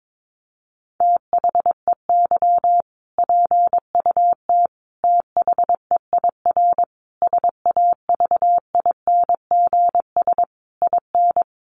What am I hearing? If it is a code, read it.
Morse code, heard as a keyed tone, about 22 wpm, 700 Hz. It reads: THEY PUT THEIR SAVINGS IN
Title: Sentences from Top 1200 Words 22wpm